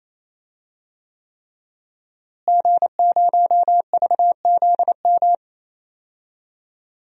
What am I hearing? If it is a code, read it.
Morse code, heard as a keyed tone, about 28 wpm, 700 Hz. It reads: G0VZM
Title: DXCC-100 Call Signs - 5 Characters - Repeated 3x - 28wpm